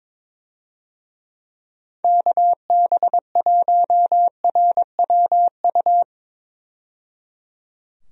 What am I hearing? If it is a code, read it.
Morse code, heard as a keyed tone, about 22 wpm, 700 Hz. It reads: KB1RWU
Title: USA Call Signs - 6 Characters - 22wpm